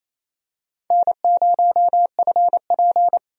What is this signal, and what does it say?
Morse code, heard as a keyed tone, about 28 wpm, 700 Hz. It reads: N0FP
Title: USA Call Signs - 4 Characters - 28wpm